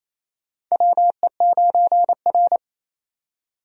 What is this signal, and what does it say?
Morse code, heard as a keyed tone, about 27 wpm, 700 Hz. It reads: WE9R
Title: USA Call Signs - 4 Characters - 28wpm